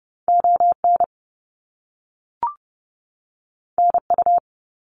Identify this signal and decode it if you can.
Morse code, heard as a keyed tone, about 30 wpm, 700 Hz. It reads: ON NU
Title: Canadian Provinces and Territories 30wpm